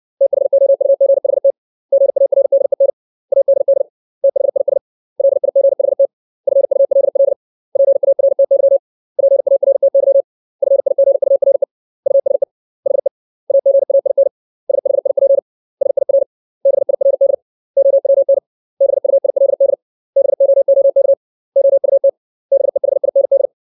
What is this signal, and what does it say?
Morse code, heard as a keyed tone, about 60 wpm, 550 Hz. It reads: THOUGHT GARDEN ADD THIS LIGHT FULL PARTY PARTY FIGURE USE HE AGAIN SHIP SIR BIRD OWN BUILD BOOK OUT BEHIND
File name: 632 words / 632 Words 60WPM